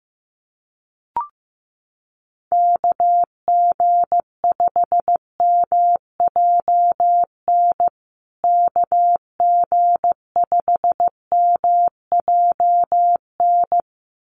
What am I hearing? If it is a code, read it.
Morse code, heard as a keyed tone, about 15 wpm, 700 Hz. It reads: KG5MJN KG5MJN
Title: USA Call Signs Repeated 2x 15wpm